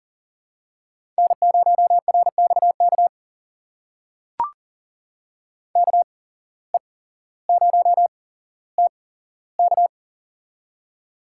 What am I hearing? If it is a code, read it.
Morse code, heard as a keyed tone, about 40 wpm, spaced as for 17 wpm, 700 Hz. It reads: N0RXK KE0TK
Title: USA Call Signs - 8x Character Spacing - 40wpm